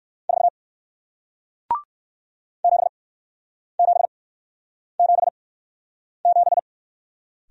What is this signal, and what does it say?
Morse code, heard as a keyed tone, about 58 wpm, spaced as for 9 wpm, 700 Hz. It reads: 4 7777